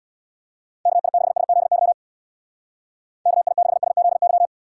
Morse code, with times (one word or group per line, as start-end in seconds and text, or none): 0.9-1.9 KI6SZY
3.3-4.5 KI6SZY